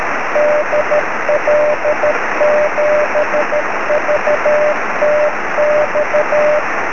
Morse code, heard as a keyed tone, about 13 wpm, 600 Hz, with noise. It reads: DL7VTX
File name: sm7lcb / 18-dl7vtx 23cm